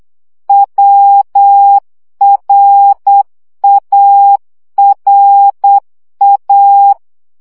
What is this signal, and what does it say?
Morse code, heard as a keyed tone, about 8 wpm, 800 Hz. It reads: WRARA
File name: CW gyakorlas a r w 7 baud